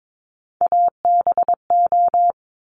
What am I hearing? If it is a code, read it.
Morse code, heard as a keyed tone, about 22 wpm, 700 Hz. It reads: ABO
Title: Top Three Letter Combinations in Words 22wpm